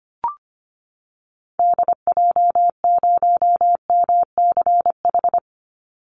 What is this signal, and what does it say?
Morse code, heard as a keyed tone, about 25 wpm, 700 Hz. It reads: DJ0MCH
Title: Call Signs Repeated 3x - 25wpm